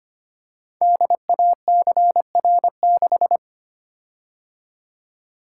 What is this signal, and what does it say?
Morse code, heard as a keyed tone, about 25 wpm, 700 Hz. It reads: DACR6
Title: Code Groups - 5 Letter:Numbers - 25wpm